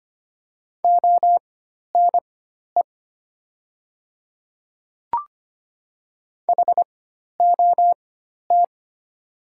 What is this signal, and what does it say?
Morse code, heard as a keyed tone, about 25 wpm, spaced as for 12 wpm, 690 Hz. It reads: ONE HOT
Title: 4x Character Spacing - Top 300 Words - 25wpm